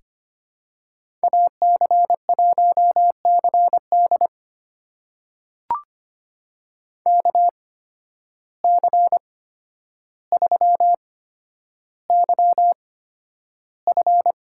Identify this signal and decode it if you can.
Morse code, heard as a keyed tone, about 25 wpm, spaced as for 11 wpm, 700 Hz. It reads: AC1CD KC3YF
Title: USA Call Signs - 5 Characters - 8x Character Spacing - 25wpm